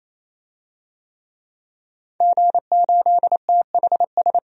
Morse code, heard as a keyed tone, about 27 wpm, 700 Hz. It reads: G8THS
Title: DXCC-100 Call Signs - 5 Characters - 28wpm